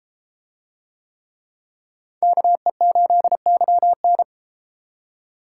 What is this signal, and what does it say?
Morse code, heard as a keyed tone, about 33 wpm, 700 Hz. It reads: KE8YN